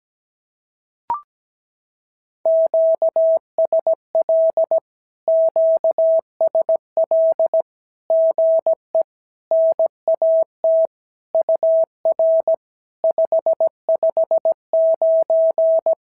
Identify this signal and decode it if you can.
Morse code, heard as a keyed tone, about 17 wpm, 650 Hz. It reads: QSL QSL GE NAT UR 559
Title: SOTA QSO Practice 17wpm